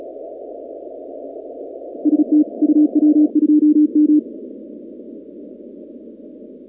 Morse code, heard as a keyed tone, about 36 wpm, 300 Hz. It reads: ST UW2M
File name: contests / 2019-cw-wpx-kt5j-k5pi- -2330 0-38